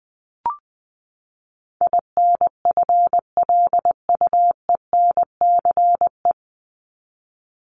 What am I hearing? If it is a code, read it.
Morse code, heard as a keyed tone, about 20 wpm, 700 Hz. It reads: INFLUENCE
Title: Top 1001-1250 Words 20wpm